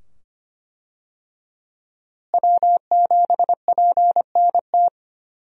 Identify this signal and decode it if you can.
Morse code, heard as a keyed tone, about 25 wpm, 700 Hz. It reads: W7PNT
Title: USA Call Signs - 5 Characters - Repeated 2x - 25wpm